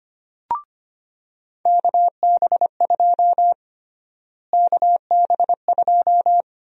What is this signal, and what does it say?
Morse code, heard as a keyed tone, about 25 wpm, 700 Hz. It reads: KB2 KB2
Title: USA Call Sign Prefixes - Repeated 2x - 25wpm